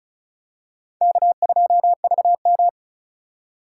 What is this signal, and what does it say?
Morse code, heard as a keyed tone, about 36 wpm, 700 Hz. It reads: K2VM
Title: USA Call Signs - 4 Characters - 2x Character Spacing - 35wpm